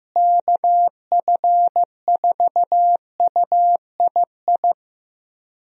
Morse code, heard as a keyed tone, about 15 wpm, 700 Hz. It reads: KF4UII
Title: USA Call Signs - 6 Characters - Repeated 2x - 15wpm